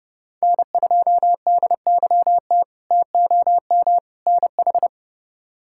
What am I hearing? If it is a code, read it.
Morse code, heard as a keyed tone, about 30 wpm, 700 Hz. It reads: N2DYT TOM NH
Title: K1USN SST QSO Practice - 30wpm